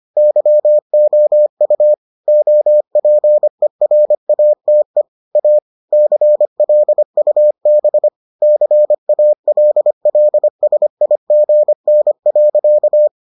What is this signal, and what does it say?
Morse code, heard as a keyed tone, about 25 wpm, 600 Hz. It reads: YOU OPERATE A CLUB CALLSIGN.